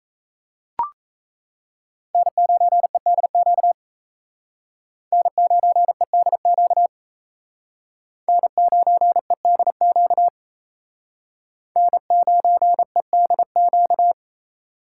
Morse code, heard as a keyed tone, about 35 wpm, 700 Hz. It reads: N9EDQ N9EDQ N9EDQ N9EDQ
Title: Speed Racing - USA 1x3 Call Signs 28wpm